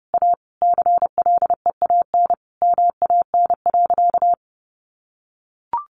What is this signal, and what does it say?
Morse code, heard as a keyed tone, about 30 wpm, 700 Hz. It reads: A CLEAN MAN.
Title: Sentences from Top 1400 Words 30wpm